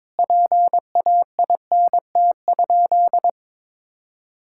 Morse code, heard as a keyed tone, about 22 wpm, 700 Hz. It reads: PAINT?
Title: Sentences from Top 1000 Words - Review Part 6 - 22wpm